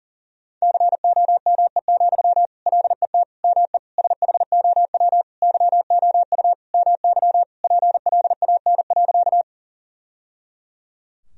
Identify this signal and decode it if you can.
Morse code, heard as a keyed tone, about 40 wpm, 700 Hz. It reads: COME, LET ME SHOW YOU MY PLAN.
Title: Sentences from Top 1000 Words - Review Part 2 - 40wpm